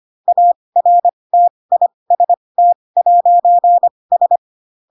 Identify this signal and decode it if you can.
Morse code, heard as a keyed tone, about 25 wpm, spaced as for 20 wpm, 700 Hz. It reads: ARTIST'S
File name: Pinocchio / Pinocchio ch8 15 25 ik1hge